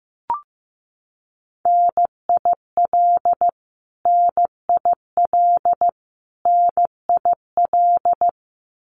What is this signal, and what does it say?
Morse code, heard as a keyed tone, about 15 wpm, 700 Hz. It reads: NIL NIL NIL
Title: Abbreviations Repeated 3x - 15wpm